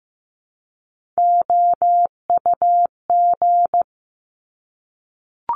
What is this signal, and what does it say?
Morse code, heard as a keyed tone, about 15 wpm, 700 Hz. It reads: OUG
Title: Top Three Letter Combinations in Words 15wpm